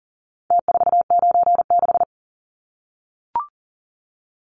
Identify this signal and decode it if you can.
Morse code, heard as a keyed tone, about 40 wpm, 700 Hz. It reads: T496
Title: Numbers - Contest Style - 4-Digit 40wpm